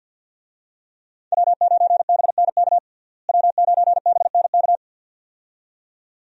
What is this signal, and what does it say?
Morse code, heard as a keyed tone, about 52 wpm, 700 Hz. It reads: W9BNK W9BNK
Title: USA Call Signs - 5 Characters - Repeated 2x - 50wpm